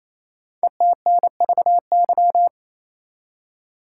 Morse code, heard as a keyed tone, about 29 wpm, 700 Hz. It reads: ETNVY
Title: Code Groups - 5 Letters - 28wpm